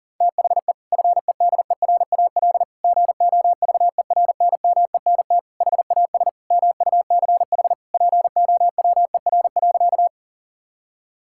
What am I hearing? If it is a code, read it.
Morse code, heard as a keyed tone, about 40 wpm, 690 Hz. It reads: THE FEDERAL GOVERNMENT HAS MUCH POWER.
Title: Sentences from Top 1000 Words - Review Part 5 - 40wpm